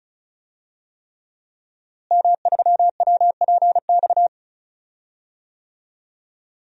Morse code, heard as a keyed tone, about 34 wpm, 700 Hz. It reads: M3WPX
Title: DXCC-100 Call Signs - 5 Characters - Repeated 3x - 35wpm